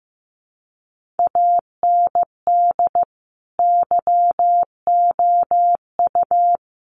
Morse code, heard as a keyed tone, about 15 wpm, 700 Hz. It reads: AND YOU
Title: Sets of 5 Words Encore2 15wpm